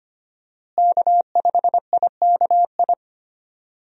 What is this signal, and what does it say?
Morse code, heard as a keyed tone, about 25 wpm, 700 Hz. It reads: K5IKI
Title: USA Call Signs - 5 Characters - Repeated 3x - 25wpm